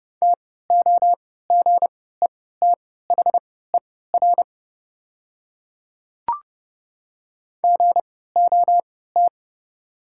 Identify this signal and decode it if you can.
Morse code, heard as a keyed tone, about 30 wpm, spaced as for 17 wpm, 690 Hz. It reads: TOGETHER GOT
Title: 3x Character Spacing - Top 300 Words - 30wpm